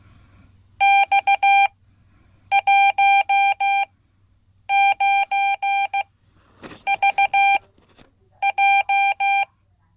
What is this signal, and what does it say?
Morse code, heard as a keyed tone, about 16 wpm, spaced as for 8 wpm, 790 Hz. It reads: X19VJ
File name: Code / 5min-b